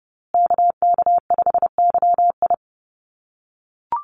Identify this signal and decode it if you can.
Morse code, heard as a keyed tone, about 30 wpm, 700 Hz. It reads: KK5YI
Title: USA Call Signs Repeated 2x 30wpm